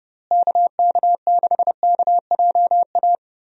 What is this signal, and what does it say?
Morse code, heard as a keyed tone, about 31 wpm, 700 Hz. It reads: KK6KJA
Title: USA Call Signs - 6 Characters - Repeated 3x - 30wpm